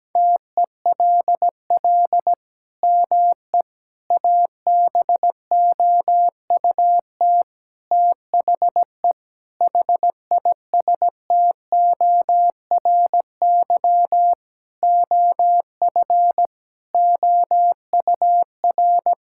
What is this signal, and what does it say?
Morse code, heard as a keyed tone, about 17 wpm, 700 Hz. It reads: TELL ME ABOUT THE HISTORY OF OUR
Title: Sentences from Top 1000 Words - Review Part 6 - 17wpm